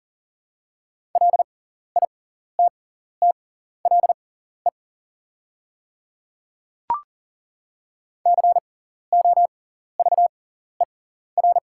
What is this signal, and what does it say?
Morse code, heard as a keyed tone, about 40 wpm, spaced as for 14 wpm, 700 Hz. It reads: LITTLE COVER
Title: 6x Character Spacing - Top 300 Words - 40wpm